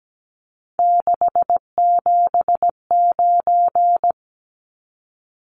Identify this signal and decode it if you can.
Morse code, heard as a keyed tone, about 17 wpm, 700 Hz. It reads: 679
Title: Numbers - 3-Digit 17wpm